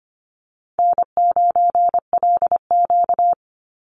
Morse code, heard as a keyed tone, about 25 wpm, 700 Hz. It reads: N9LQ
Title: Easy Call Signs Repeated 3x - 25wpm